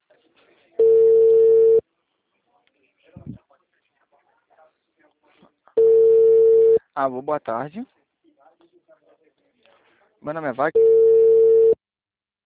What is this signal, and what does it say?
Ringback tone (China, 450 Hz), with speech.